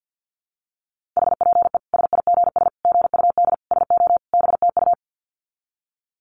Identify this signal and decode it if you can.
Morse code, heard as a keyed tone, about 48 wpm, 700 Hz. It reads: HPE SIGS GUD SO BTU
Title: Common QSO Phrases 50wpm